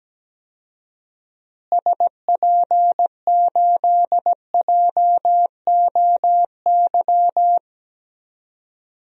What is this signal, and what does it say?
Morse code, heard as a keyed tone, about 17 wpm, 700 Hz. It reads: SP8JOY
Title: DXCC-100 Call Signs - 6 Characters - Repeated 2x - 17wpm